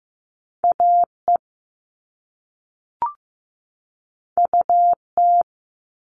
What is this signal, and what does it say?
Morse code, heard as a keyed tone, about 15 wpm, 700 Hz. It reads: AE UT
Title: All Cut Numbers - 2-Digit 15wpm